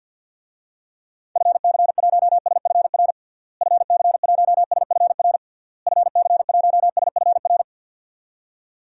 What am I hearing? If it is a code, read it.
Morse code, heard as a keyed tone, about 52 wpm, 700 Hz. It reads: FC1SFR FC1SFR FC1SFR
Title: DXCC-100 Call Signs - 6 Characters - Repeated 3x - 50wpm